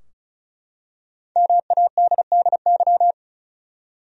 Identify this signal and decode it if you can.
Morse code, heard as a keyed tone, about 36 wpm, 700 Hz. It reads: MADDY
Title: USA Common Names - Max 5 Letters - 35wpm